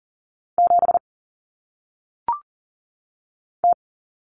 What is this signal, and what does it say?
Morse code, heard as a keyed tone, about 40 wpm, 700 Hz. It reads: 7 T
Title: Common Cut Numbers - 1-Digit 40wpm